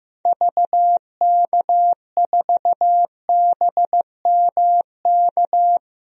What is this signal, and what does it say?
Morse code, heard as a keyed tone, about 15 wpm, 700 Hz. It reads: VK4BMK
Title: DXCC-100 Call Signs - 6 Characters - 15wpm